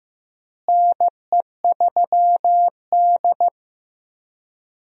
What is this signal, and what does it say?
Morse code, heard as a keyed tone, about 15 wpm, 700 Hz. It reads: NE3D
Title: USA Call Signs - 4 Characters - Repeated 2x - 15wpm